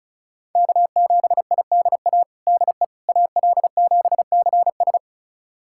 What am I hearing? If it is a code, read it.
Morse code, heard as a keyed tone, about 34 wpm, 700 Hz. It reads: K7IDA DE AL7CS
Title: USA Callsign DE Callsign - 5 Characters - 35wpm